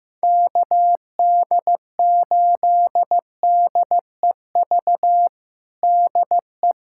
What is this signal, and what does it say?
Morse code, heard as a keyed tone, about 15 wpm, 700 Hz. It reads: KD8DEV DE
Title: USA Callsign DE Callsign 15wpm